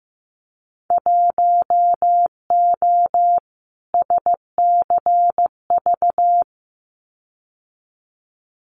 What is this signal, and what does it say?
Morse code, heard as a keyed tone, about 15 wpm, 700 Hz. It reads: 1O SCV
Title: Winter Field Day Exchanges 15wpm